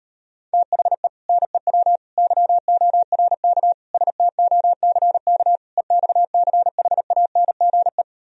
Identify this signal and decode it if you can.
Morse code, heard as a keyed tone, about 38 wpm, 700 Hz. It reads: THE NEW YORK STOCK EXCHANGE